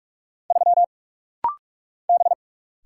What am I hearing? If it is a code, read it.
Morse code, heard as a keyed tone, about 45 wpm, 700 Hz. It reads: 3 B
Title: Single Letter-Number Alt - Rapid-Fire 45wpm